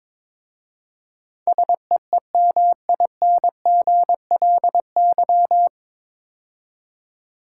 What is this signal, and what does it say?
Morse code, heard as a keyed tone, about 22 wpm, 700 Hz. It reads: SEEMINGLY
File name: 9 Letter Words - First 3 Letters Given - 22wpm